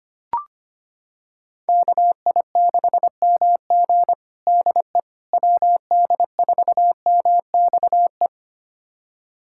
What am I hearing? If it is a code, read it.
Morse code, heard as a keyed tone, about 25 wpm, 700 Hz. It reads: KI6MG DE WD4MXE
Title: USA Callsign DE Callsign 25wpm